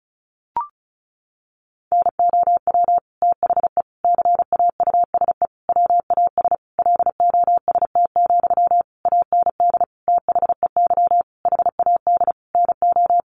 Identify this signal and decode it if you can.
Morse code, heard as a keyed tone, about 34 wpm, 700 Hz. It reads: NOW THE CAUSE WAS LOST, AND THEY HAD NO
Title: Sentences from Top 1500 Words 35wpm